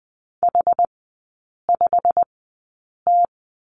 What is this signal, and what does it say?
Morse code, heard as a keyed tone, about 20 wpm, spaced as for 8 wpm, 700 Hz. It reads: H5T
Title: 145 Lesson 24e - Letter y - Three Characters 20wpm